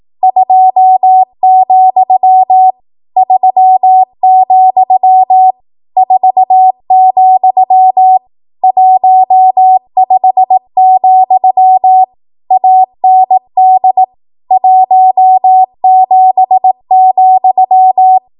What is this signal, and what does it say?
Morse code, heard as a keyed tone, about 18 wpm, 750 Hz. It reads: 2, 3, 4, 15, AND 17,
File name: Archive / 18 WPM / 18WPM